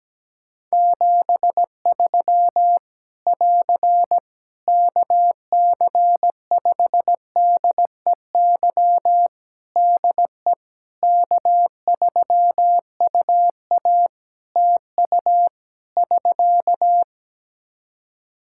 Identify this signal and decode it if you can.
Morse code, heard as a keyed tone, about 17 wpm, 700 Hz. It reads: 73 <AR> KC5DEY DE K3UA TU <SK>
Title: Practice QSO - Exchange 4 - Ending - 17wpm